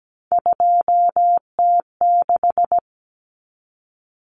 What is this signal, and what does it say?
Morse code, heard as a keyed tone, about 17 wpm, 700 Hz. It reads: 2T6
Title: Common Cut Numbers - 3-Digit 17wpm